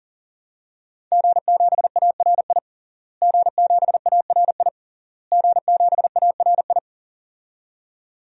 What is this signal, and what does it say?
Morse code, heard as a keyed tone, about 40 wpm, 700 Hz. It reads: G7ARI G7ARI G7ARI
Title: DXCC-100 Call Signs - 5 Characters - Repeated 3x - 40wpm